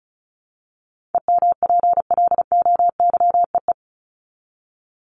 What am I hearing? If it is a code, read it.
Morse code, heard as a keyed tone, about 34 wpm, 700 Hz. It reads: EMPLOYEE
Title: Top 1501-1750 Words 35wpm